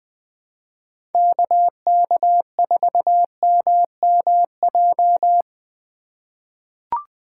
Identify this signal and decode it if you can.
Morse code, heard as a keyed tone, about 20 wpm, 700 Hz. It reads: KK4MMJ
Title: USA Call Signs - 6 Characters - 2x Character Spacing - 20wpm